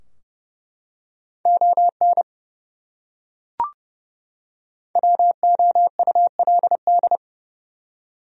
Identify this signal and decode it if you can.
Morse code, heard as a keyed tone, about 32 wpm, 700 Hz. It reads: ON WOULD
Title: Top 100 Words 30wpm